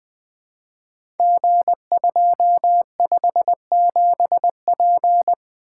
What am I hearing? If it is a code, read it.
Morse code, heard as a keyed tone, about 20 wpm, 700 Hz. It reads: G257P
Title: Code Groups - 5 Letter:Numbers - 20wpm